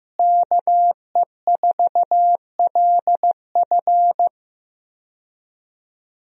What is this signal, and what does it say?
Morse code, heard as a keyed tone, about 15 wpm, 700 Hz. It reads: KE4LF
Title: USA Call Signs - 5 Characters - Repeated 2x - 15wpm